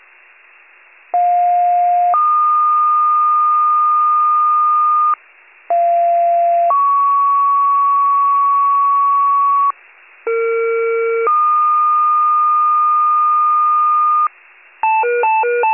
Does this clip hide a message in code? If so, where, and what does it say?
Yes: two-tone page from 1.1 to 14.3 s, heard as a one-second tone, then a three-second tone, about 710 then 1220 Hz, about 690 then 1120 Hz, about 460 then 1220 Hz.